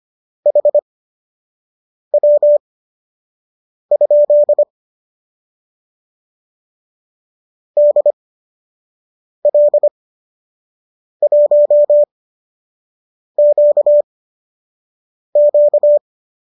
Morse code, heard as a keyed tone, about 25 wpm, spaced as for 6 wpm, 590 Hz. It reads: HW? DL1QQ
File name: qso012 06